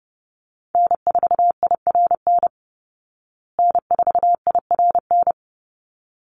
Morse code, heard as a keyed tone, about 30 wpm, 700 Hz. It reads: N4IRN N4IRN
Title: Call Signs Repeated 3x - 30wpm